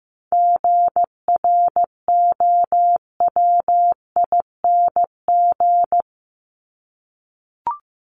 Morse code, heard as a keyed tone, about 15 wpm, 700 Hz. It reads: GROWING
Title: ING Suffix - Top 100 Words with Base Words 15wpm